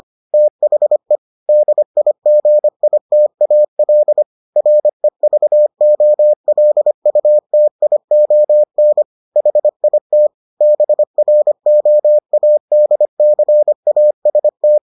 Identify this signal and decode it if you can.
Morse code, heard as a keyed tone, about 25 wpm, 600 Hz. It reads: THE DIGITAL REVOLUTION HIT BROADCAST